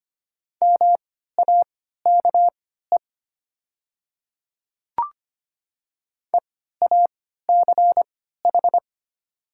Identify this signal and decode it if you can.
Morse code, heard as a keyed tone, about 25 wpm, spaced as for 14 wpm, 700 Hz. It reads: MAKE EACH
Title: 3x Character Spacing - Top 100 Words - 25wpm